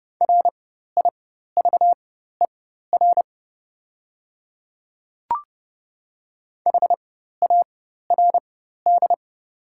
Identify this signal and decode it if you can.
Morse code, heard as a keyed tone, about 30 wpm, spaced as for 14 wpm, 690 Hz. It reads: RIVER HARD
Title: 4x Character Spacing - Top 500 Words - 30wpm